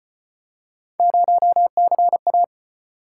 Morse code, heard as a keyed tone, about 36 wpm, 700 Hz. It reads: 0CA